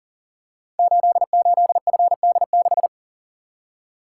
Morse code, heard as a keyed tone, about 42 wpm, 690 Hz. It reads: 88FD6
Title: Code Groups - 5 Letter:Numbers - 40wpm